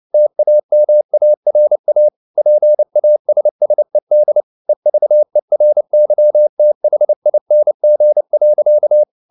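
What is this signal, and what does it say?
Morse code, heard as a keyed tone, about 29 wpm, 600 Hz. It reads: TAMARA PASSED EVERYTHING.
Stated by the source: Morse code at 29 wpm, sending Ken Follett's novel Never in English, from CW med Gnister.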